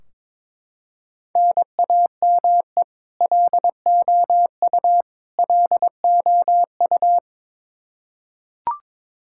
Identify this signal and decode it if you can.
Morse code, heard as a keyed tone, about 22 wpm, 700 Hz. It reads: NAME LOU LOU
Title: Practice QSO - Exchange 1 - Beginning - 22wpm